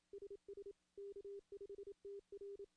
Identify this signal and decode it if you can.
Morse code, heard as a keyed tone, about 27 wpm, 400 Hz. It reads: SS K5TR
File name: contests / CW-SS-k5tr- -1600